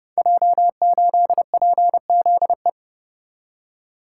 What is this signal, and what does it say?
Morse code, heard as a keyed tone, about 30 wpm, 700 Hz. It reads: J8PZE
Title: Code Groups - 5 Letter:Numbers - 30wpm